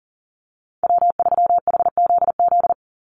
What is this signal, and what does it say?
Morse code, heard as a keyed tone, about 40 wpm, 700 Hz. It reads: W3HZZ
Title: Call Signs Repeated 3x - 40wpm